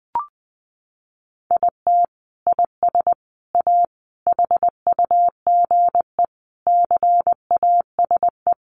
Morse code, heard as a keyed tone, about 20 wpm, 700 Hz. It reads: IT IS A HUGE CASE
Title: Sentences from Top 1100 Words 20wpm